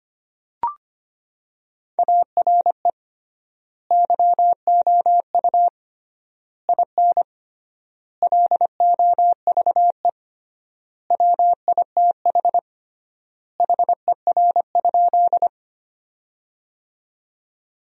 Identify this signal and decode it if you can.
Morse code, heard as a keyed tone, about 25 wpm, 700 Hz. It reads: ARE YOU IN LOVE WITH HER?
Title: 3x Word-Spacing - Sentences from Top 400 Words - 25wpm